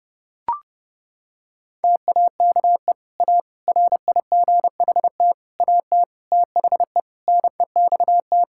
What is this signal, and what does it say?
Morse code, heard as a keyed tone, about 30 wpm, 700 Hz. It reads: TAKE A RIGHT AT THE NEXT
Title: Sentences from Top 1600 Words 30wpm